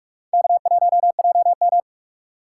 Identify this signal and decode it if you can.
Morse code, heard as a keyed tone, about 44 wpm, 700 Hz. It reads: K1JM
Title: USA Call Signs - 4 Characters - 6x Character Spacing - 45wpm